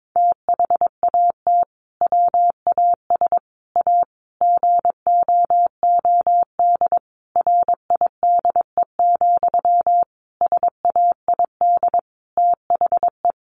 Morse code, heard as a keyed tone, about 22 wpm, 700 Hz. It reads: THAT WAS A GOOD RIDE, SAID THE
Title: Sentences from Top 1200 Words 22wpm